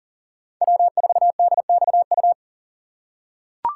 Morse code, heard as a keyed tone, about 40 wpm, 700 Hz. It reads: W4DXU
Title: USA Call Signs - 5 Characters - 3x Character Spacing - 40wpm